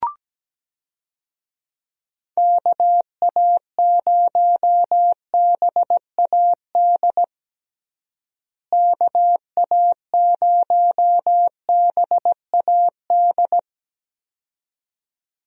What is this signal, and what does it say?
Morse code, heard as a keyed tone, about 17 wpm, 700 Hz. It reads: KA0BAD KA0BAD
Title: USA Call Signs - 6 Characters - Repeated 2x - 17wpm